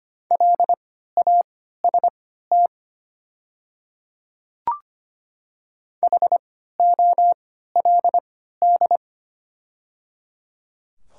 Morse code, heard as a keyed tone, about 25 wpm, spaced as for 14 wpm, 700 Hz. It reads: LAST HOLD
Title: 3x Character Spacing - Top 500 Words - 25wpm